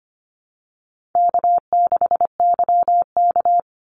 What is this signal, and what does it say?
Morse code, heard as a keyed tone, about 25 wpm, 700 Hz. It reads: K6YK
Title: NA SOTA Chaser Callsigns 25wpm